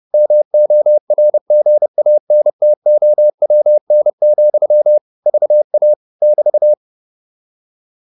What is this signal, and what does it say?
Morse code, heard as a keyed tone, about 30 wpm, 600 Hz. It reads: MORGANTOWN, VA <BT>
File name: QSO302 30